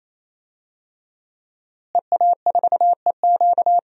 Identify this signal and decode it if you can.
Morse code, heard as a keyed tone, about 28 wpm, 700 Hz. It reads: EA4EQ
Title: DXCC-100 Call Signs - 5 Characters - Repeated 3x - 28wpm